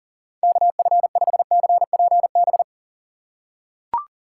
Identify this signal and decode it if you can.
Morse code, heard as a keyed tone, about 40 wpm, 700 Hz. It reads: KF5CPB
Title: USA Call Signs - 6 Characters - 4x Character Spacing - 40wpm